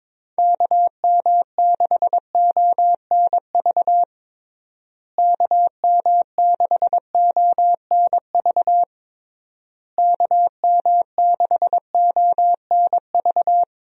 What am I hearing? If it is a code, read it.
Morse code, heard as a keyed tone, about 22 wpm, 700 Hz. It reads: KM6ONV KM6ONV KM6ONV
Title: USA Call Signs - 6 Characters - Repeated 3x - 22wpm